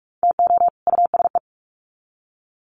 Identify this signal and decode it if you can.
Morse code, heard as a keyed tone, about 44 wpm, 700 Hz. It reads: TO USE
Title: Sets of 2 Words 45wpm